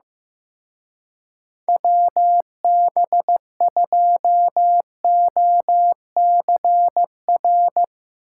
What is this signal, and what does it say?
Morse code, heard as a keyed tone, about 15 wpm, 700 Hz. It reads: WB2OCR
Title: Speed Racing - USA 2x3 Call Signs 15wpm